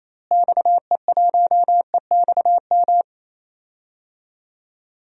Morse code, heard as a keyed tone, about 28 wpm, 700 Hz. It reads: XE1EXM
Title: DXCC-100 Call Signs - 6 Characters - Repeated 3x - 28wpm